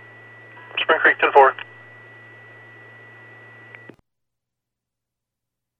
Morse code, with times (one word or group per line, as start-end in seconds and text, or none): none